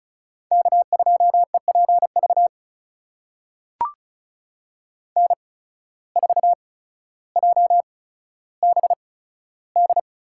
Morse code, heard as a keyed tone, about 34 wpm, spaced as for 15 wpm, 700 Hz. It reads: K2EPV N4JBD
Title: USA Call Signs - 5 Characters - 8x Character Spacing - 35wpm